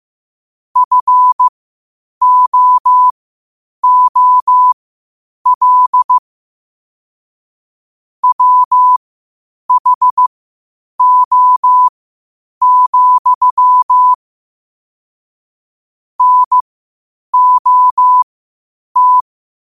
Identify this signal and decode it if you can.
Morse code, heard as a keyed tone, about 15 wpm, spaced as for 8 wpm, 1000 Hz. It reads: FOOL WHO, NOT
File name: qotdfiles / 8wpm